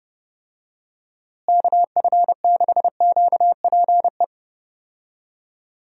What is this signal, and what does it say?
Morse code, heard as a keyed tone, about 30 wpm, 700 Hz. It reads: KF6QPE
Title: USA Call Signs - 6 Characters - Repeated 3x - 30wpm